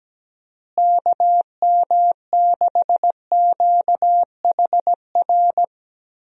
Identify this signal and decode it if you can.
Morse code, heard as a keyed tone, about 17 wpm, 700 Hz. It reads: KM6QHR
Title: USA Call Signs - 6 Characters - Repeated 2x - 17wpm